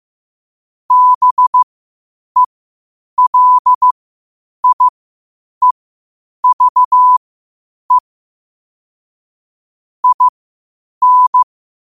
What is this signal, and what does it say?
Morse code, heard as a keyed tone, about 15 wpm, spaced as for 8 wpm, 1000 Hz. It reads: BELIEVE IN
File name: qotdfiles / 8wpm